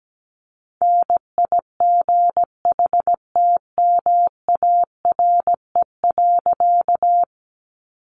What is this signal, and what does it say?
Morse code, heard as a keyed tone, about 17 wpm, 700 Hz. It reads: NIGHTMARE.
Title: A Sound Of Thunder 17wpm